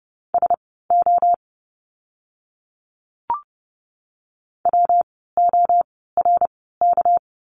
Morse code, heard as a keyed tone, about 30 wpm, spaced as for 17 wpm, 700 Hz. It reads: SO WORK
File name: course / 30:10wpm / 230 Lesson 37f - Letter x - Words 30wpm